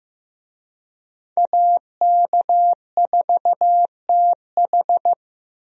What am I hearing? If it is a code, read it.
Morse code, heard as a keyed tone, about 15 wpm, 700 Hz. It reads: AK4TH
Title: USA Call Signs - 5 Characters - Repeated 3x - 15wpm